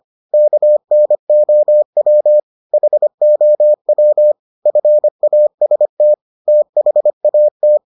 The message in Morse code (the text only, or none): KNOW HOW FAST THAT